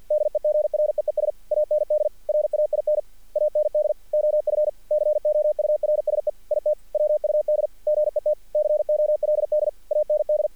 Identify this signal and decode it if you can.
Morse code, heard as a keyed tone, about 46 wpm, 610 Hz. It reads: DEGREES AND RAIN AND OF COURSE IT WUD GET COLD AND
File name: archived / CRSnet